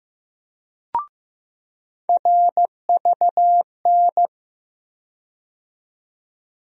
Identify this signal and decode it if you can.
Morse code, heard as a keyed tone, about 15 wpm, 700 Hz. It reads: RVN
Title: USA Call Sign Postfixes - 3 Letters 15wpm